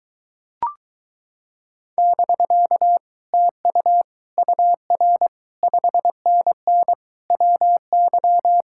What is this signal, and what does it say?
Morse code, heard as a keyed tone, about 22 wpm, 700 Hz. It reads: <BK> TU UR 5NN WY